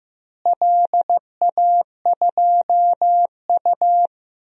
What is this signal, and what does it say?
Morse code, heard as a keyed tone, about 15 wpm, 700 Hz. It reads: LA2U
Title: DXCC-100 Call Signs - 4 Characters - Repeated 2x - 15wpm